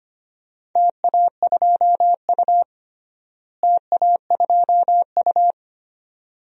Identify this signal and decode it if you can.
Morse code, heard as a keyed tone, about 25 wpm, 700 Hz. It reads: TA2U TA2U
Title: DXCC-100 Call Signs - 4 Characters - Repeated 2x - 25wpm